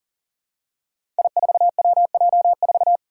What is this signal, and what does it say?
Morse code, heard as a keyed tone, about 40 wpm, 700 Hz. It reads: I4WJ4
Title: Five Challenging Letter-Number Combinations 40wpm